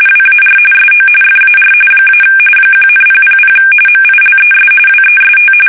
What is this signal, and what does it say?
RTTY, about 45.45 baud. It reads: WAR SO FAR - 1,000 MILES WEST
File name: fsk850 / battleshipnj